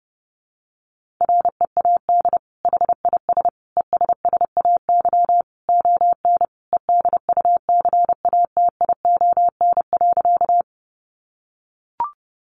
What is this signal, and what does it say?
Morse code, heard as a keyed tone, about 30 wpm, 700 Hz. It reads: READ HIS ESSAY ON EDUCATION.
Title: Sentences from Top 1500 Words 30wpm